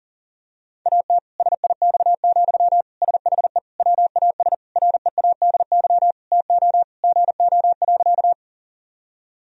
Morse code, heard as a keyed tone, about 40 wpm, 700 Hz. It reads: AT SIX, SHE WAS READY TO GO.
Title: Sentences from Top 1000 Words - Review Part 2 - 40wpm